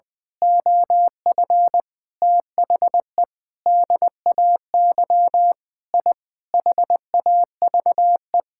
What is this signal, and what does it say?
Morse code, heard as a keyed tone, about 20 wpm, 700 Hz. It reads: OF THE DAY I HAVE
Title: Sentences with 8-Word Phrases - Spoken Beginning - 20wpm